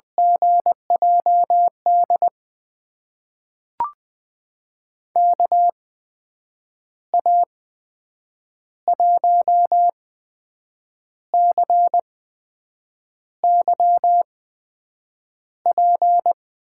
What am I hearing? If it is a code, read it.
Morse code, heard as a keyed tone, about 20 wpm, spaced as for 7 wpm, 700 Hz. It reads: GJD KA1CYP
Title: USA Call Signs - 6 Characters - 8x Character Spacing - 20wpm